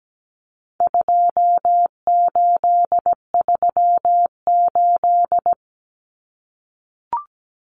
Morse code, heard as a keyed tone, about 17 wpm, 700 Hz. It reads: 2838
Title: Common Cut Numbers - 4-Digit 17wpm